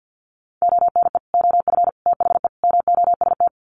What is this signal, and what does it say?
Morse code, heard as a keyed tone, about 50 wpm, 700 Hz. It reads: ONE OF THE MOST
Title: Sets of 4 Words 50wpm